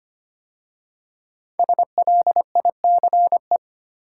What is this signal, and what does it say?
Morse code, heard as a keyed tone, about 25 wpm, 700 Hz. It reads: SLICE
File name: 5 Letter Words - First 3 Letters Given - 25wpm